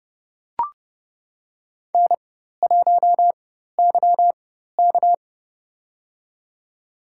Morse code, heard as a keyed tone, about 30 wpm, spaced as for 14 wpm, 700 Hz. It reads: N1YK
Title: USA Call Signs - 4x Character Spacing - 30wpm